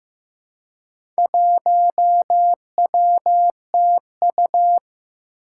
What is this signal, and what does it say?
Morse code, heard as a keyed tone, about 16 wpm, 700 Hz. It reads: 1WTU
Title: Four Challenging Letter-Number Combinations 15wpm